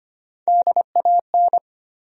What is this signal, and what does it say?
Morse code, heard as a keyed tone, about 25 wpm, 700 Hz. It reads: DAN
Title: USA Common Names - Max 4 Letters - 25wpm